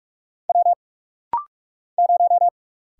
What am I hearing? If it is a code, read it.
Morse code, heard as a keyed tone, about 44 wpm, 700 Hz. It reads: W 0
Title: Single Letter-Number Alt - Rapid-Fire 45wpm